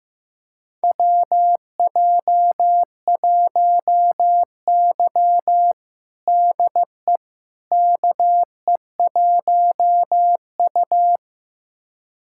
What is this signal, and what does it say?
Morse code, heard as a keyed tone, about 15 wpm, 700 Hz. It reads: WJ1Y DE KE1U
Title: USA Callsign DE Callsign - 4 Characters - 15wpm